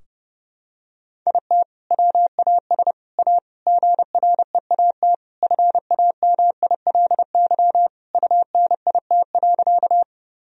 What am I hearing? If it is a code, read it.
Morse code, heard as a keyed tone, about 30 wpm, 700 Hz. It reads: IT WAS A GREAT FAMILY UNIT.
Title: Sentences from Top 1000 Words - Review Part 2 - 30wpm